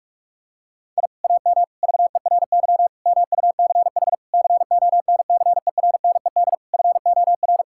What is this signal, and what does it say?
Morse code, heard as a keyed tone, about 46 wpm, 700 Hz. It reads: I AM VERY MUCH CONCERNED FOR
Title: Sentences from Top 1000 Words - Review Part 6 - 45wpm